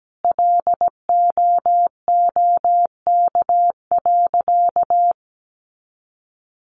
Morse code, heard as a keyed tone, about 17 wpm, 700 Hz. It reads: LOOK.
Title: Sentences from Top 300 Words 17wpm